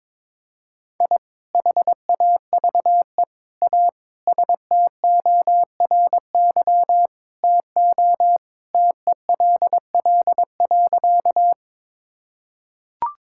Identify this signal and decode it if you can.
Morse code, heard as a keyed tone, about 22 wpm, 700 Hz. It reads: I HAVE A STORY TO TELL.
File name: practice / 22wpm / Sentences with 2-Word Phrases - 22wpm